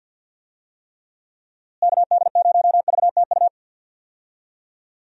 Morse code, heard as a keyed tone, about 48 wpm, 700 Hz. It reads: KD0VTU
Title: USA Call Signs - 6 Characters - Repeated 3x - 50wpm